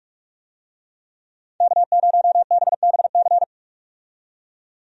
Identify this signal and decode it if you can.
Morse code, heard as a keyed tone, about 46 wpm, 700 Hz. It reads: K0BBC
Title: USA Call Signs - 5 Characters - 45wpm